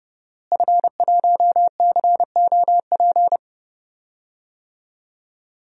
Morse code, heard as a keyed tone, about 30 wpm, 700 Hz. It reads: F1COP
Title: DXCC-100 Call Signs - 5 Characters - Repeated 3x - 30wpm